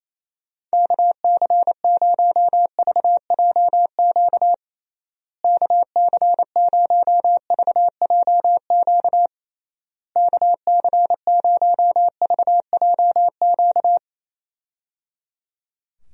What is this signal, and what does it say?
Morse code, heard as a keyed tone, about 27 wpm, 700 Hz. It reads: KC0VJQ KC0VJQ KC0VJQ
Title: USA Call Signs - 6 Characters - Repeated 3x - 28wpm